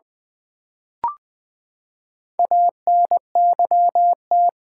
Morse code, heard as a keyed tone, about 20 wpm, 700 Hz. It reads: ANYT
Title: Build A Word - 4 Letters At A Time - 8-Letter Words - 20wpm